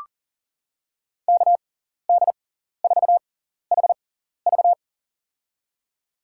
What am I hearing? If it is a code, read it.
Morse code, heard as a keyed tone, about 40 wpm, spaced as for 14 wpm, 700 Hz. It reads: KD4HV
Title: USA Call Signs - 5 Characters - 6x Character Spacing - 40wpm